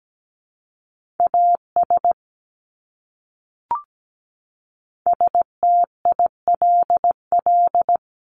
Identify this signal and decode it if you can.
Morse code, heard as a keyed tone, about 17 wpm, 700 Hz. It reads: AS STILL
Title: Top 300 Words Repeated 17wpm